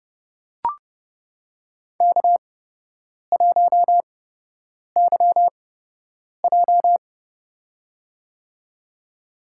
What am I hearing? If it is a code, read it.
Morse code, heard as a keyed tone, about 30 wpm, spaced as for 8 wpm, 700 Hz. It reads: K1YJ
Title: USA Call Signs - 4 Characters - 8x Character Spacing - 30wpm